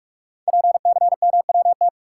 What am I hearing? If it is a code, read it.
Morse code, heard as a keyed tone, about 44 wpm, 700 Hz. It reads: PCMWT
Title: Code Groups - 5 Letters - 45wpm